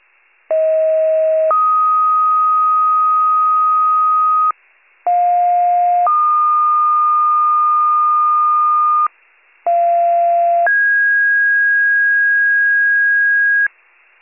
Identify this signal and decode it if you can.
Two-tone page, heard as a one-second tone, then a three-second tone, about 640 then 1250 Hz, about 710 then 1220 Hz, about 690 then 1650 Hz. It is heard from 0.5 to 13.7 s.